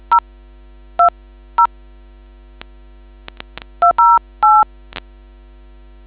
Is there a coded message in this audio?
Yes: DTMF from 0.1 to 4.6 s, heard as telephone touch tones, tones of about 120 ms. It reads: 020 208